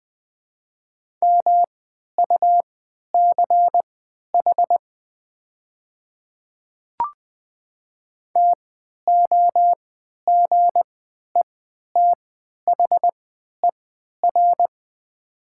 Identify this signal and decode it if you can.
Morse code, heard as a keyed tone, about 20 wpm, spaced as for 11 wpm, 690 Hz. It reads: MUCH TOGETHER
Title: 3x Character Spacing - Top 500 Words - 20wpm